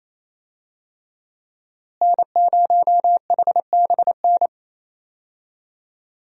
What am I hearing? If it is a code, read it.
Morse code, heard as a keyed tone, about 28 wpm, 700 Hz. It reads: N0HBN
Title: USA Call Signs - 5 Characters - Repeated 3x - 28wpm